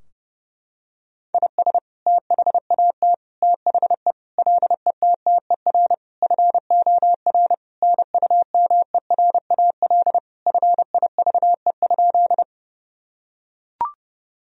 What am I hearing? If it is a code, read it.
Morse code, heard as a keyed tone, about 30 wpm, 700 Hz. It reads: IS THAT THE LETTER FOR NUMERAL FIVE?
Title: Sentences from Top 1000 Words - Review Part 2 - 30wpm